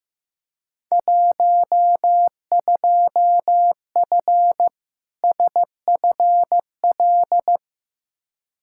Morse code, heard as a keyed tone, about 15 wpm, 700 Hz. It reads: 12F SFL
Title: Field Day Exchanges Encore - 15wpm